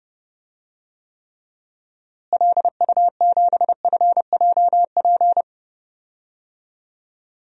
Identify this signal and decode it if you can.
Morse code, heard as a keyed tone, about 30 wpm, 700 Hz. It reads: LU7FJP
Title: DXCC-100 Call Signs - 6 Characters - Repeated 3x - 30wpm